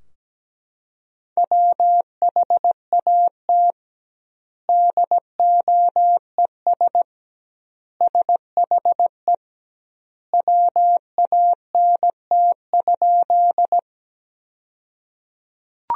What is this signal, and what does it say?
Morse code, heard as a keyed tone, about 17 wpm, 700 Hz. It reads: WHAT DOES SHE WANT?
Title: 2x Word-Spacing - Sentences from Top 200 Words - 17wpm